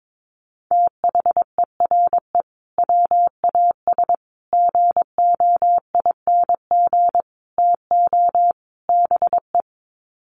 Morse code, heard as a keyed tone, about 22 wpm, 700 Hz. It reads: THERE WAS GOING TO BE
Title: Sets of 5 Words Encore2 22wpm